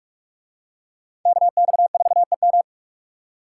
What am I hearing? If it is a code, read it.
Morse code, heard as a keyed tone, about 42 wpm, 700 Hz. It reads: KX4EM